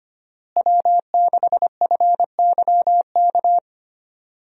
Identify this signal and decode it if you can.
Morse code, heard as a keyed tone, about 25 wpm, 700 Hz. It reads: W6FYK
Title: USA Call Signs - 5 Characters - Repeated 2x - 25wpm